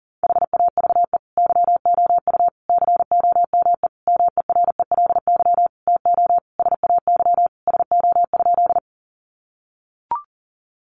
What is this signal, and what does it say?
Morse code, heard as a keyed tone, about 40 wpm, 700 Hz. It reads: HAVE YOU COME MERELY TO SAY SO?
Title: Sentences from Top 1400 Words 40wpm